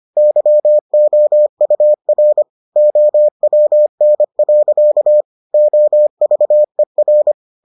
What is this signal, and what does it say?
Morse code, heard as a keyed tone, about 25 wpm, 600 Hz. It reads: YOUR OWN. OVER